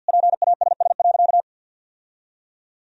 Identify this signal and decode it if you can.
Morse code, heard as a keyed tone, about 50 wpm, 700 Hz. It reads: PASS.
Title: Sentences from Top 1000 Words - Review Part 2 - 50wpm